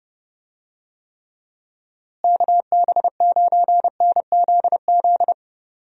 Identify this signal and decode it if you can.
Morse code, heard as a keyed tone, about 31 wpm, 700 Hz. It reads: KB9NZZ
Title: USA Call Signs - 6 Characters - Repeated 2x - 30wpm